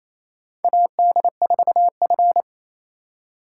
Morse code, heard as a keyed tone, about 28 wpm, 700 Hz. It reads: AD4F
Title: USA Call Signs - 4 Characters - 2x Character Spacing - 28wpm